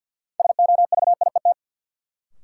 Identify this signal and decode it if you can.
Morse code, heard as a keyed tone, about 53 wpm, 700 Hz. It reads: SOVIET
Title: Sets of 5 Words Encore - New Word Drill - 50wpm